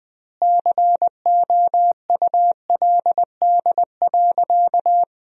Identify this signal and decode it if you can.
Morse code, heard as a keyed tone, about 21 wpm, 700 Hz. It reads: COULD.
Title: 2x Word-Spacing - Sentences from Top 500 Words - 20wpm